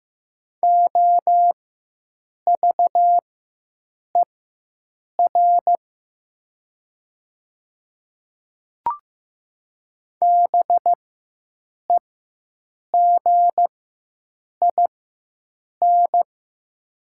Morse code, heard as a keyed tone, about 15 wpm, spaced as for 7 wpm, 700 Hz. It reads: OVER BEGIN
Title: 4x Character Spacing - Top 300 Words - 15wpm